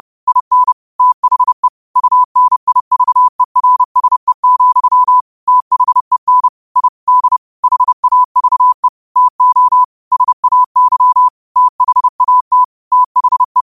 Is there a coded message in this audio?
Yes: Morse code from 0.3 to 13.6 s, heard as a keyed tone, about 30 wpm, 1000 Hz. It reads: IN THE UNIVERSE, THEN I D HAVE TO SAY THAT THE